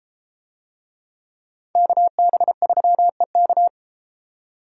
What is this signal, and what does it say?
Morse code, heard as a keyed tone, about 31 wpm, 700 Hz. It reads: KB3EK